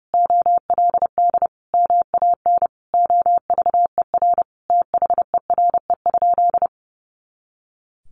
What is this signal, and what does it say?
Morse code, heard as a keyed tone, about 30 wpm, 700 Hz. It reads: OLD MAN OVER THERE?
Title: Sentences from Top 200 Words 30wpm